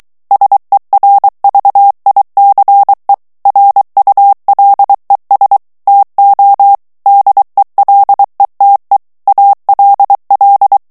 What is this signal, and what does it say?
Morse code, heard as a keyed tone, about 23 wpm, 800 Hz. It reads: SERVICE RULES TO DELETE ALL
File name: ANGLAIS texte clair 3 vitesse 22 mots minute